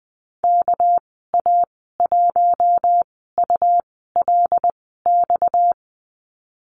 Morse code, heard as a keyed tone, about 19 wpm, spaced as for 14 wpm, 700 Hz. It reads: KA1ULX
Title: 231 Lesson 37g - Letter x - Call Signs 20wpm